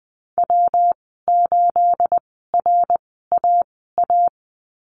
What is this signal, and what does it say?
Morse code, heard as a keyed tone, about 20 wpm, spaced as for 14 wpm, 700 Hz. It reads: W8RAA
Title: 219 Lesson 35g - Number 0 - Call Signs 20wpm